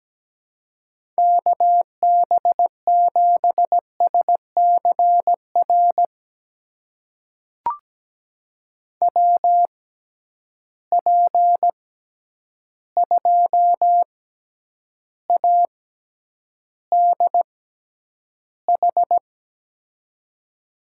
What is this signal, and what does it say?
Morse code, heard as a keyed tone, about 17 wpm, spaced as for 9 wpm, 700 Hz. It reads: KB7SCR WP2ADH
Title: USA Call Signs - 6 Characters - 6x Character Spacing - 17wpm